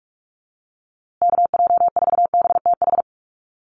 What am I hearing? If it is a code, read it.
Morse code, heard as a keyed tone, about 44 wpm, 690 Hz. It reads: KJ4BTH
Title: USA Call Signs Repeated 3x 45wpm